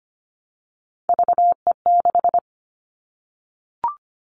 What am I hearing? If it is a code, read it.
Morse code, heard as a keyed tone, about 25 wpm, 700 Hz. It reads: VE6
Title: All Cut Numbers - 3-Digit 25wpm